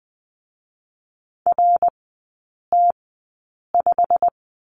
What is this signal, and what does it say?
Morse code, heard as a keyed tone, about 20 wpm, spaced as for 8 wpm, 700 Hz. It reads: RT5
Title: 107 Lesson 18e - Letter m - Three Characters 20wpm